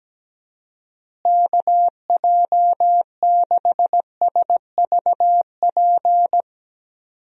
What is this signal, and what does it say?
Morse code, heard as a keyed tone, about 17 wpm, 700 Hz. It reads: KJ6SVP
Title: USA Call Signs - 6 Characters - Repeated 2x - 17wpm